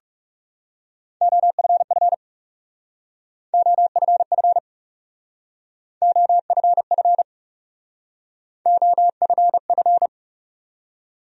Morse code, heard as a keyed tone, about 39 wpm, 700 Hz. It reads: OFF OFF OFF OFF